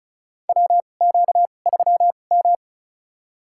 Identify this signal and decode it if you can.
Morse code, heard as a keyed tone, about 36 wpm, spaced as for 25 wpm, 700 Hz. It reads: WQ3M
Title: USA Call Signs - 4 Characters - 2x Character Spacing - 35wpm